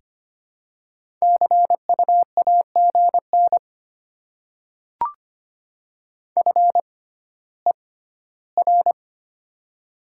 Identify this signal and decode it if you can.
Morse code, heard as a keyed tone, about 25 wpm, spaced as for 14 wpm, 700 Hz. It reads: CU AGN FER
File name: Top 20 QSO Elements - 6x Character Spacing - 25wpm